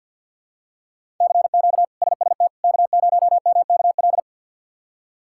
Morse code, heard as a keyed tone, about 52 wpm, 700 Hz. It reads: CQ SST K0MKL
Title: K1USN SST QSO Practice - 50wpm